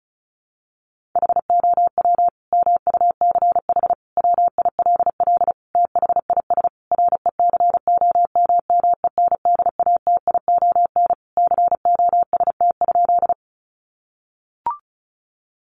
Morse code, heard as a keyed tone, about 34 wpm, 700 Hz. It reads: HOW MUCH WILL THIS RECOMMENDATION COST?
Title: Sentences from Top 1400 Words 35wpm